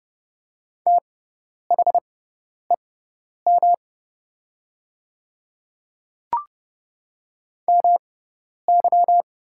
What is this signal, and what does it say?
Morse code, heard as a keyed tone, about 30 wpm, spaced as for 10 wpm, 700 Hz. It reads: THEM MY
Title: 6x Character Spacing - Top 100 Words - 30wpm